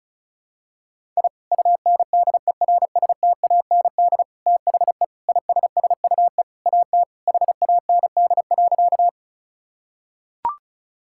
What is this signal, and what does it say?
Morse code, heard as a keyed tone, about 34 wpm, 700 Hz. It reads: I UNDERSTAND THE ISSUE AT HAND.
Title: Sentences from Top 1000 Words - Review Part 6 - 35wpm